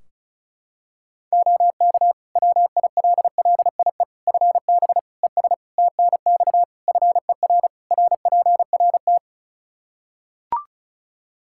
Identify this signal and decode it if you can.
Morse code, heard as a keyed tone, about 36 wpm, 700 Hz. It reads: OK WILLIE FB ES TNX FER RPRT
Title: Practice QSO - Exchange 2 - Info - 35wpm